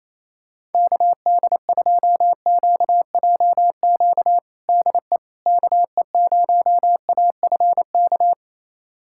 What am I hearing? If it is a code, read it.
Morse code, heard as a keyed tone, about 27 wpm, 700 Hz. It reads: KD2QJQ DE KE0AFK
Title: USA Callsign DE Callsign 28wpm